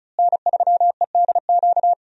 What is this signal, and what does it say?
Morse code, heard as a keyed tone, about 36 wpm, 700 Hz. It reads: N3EDQ
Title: USA Call Signs - 5 Characters - Repeated 2x - 35wpm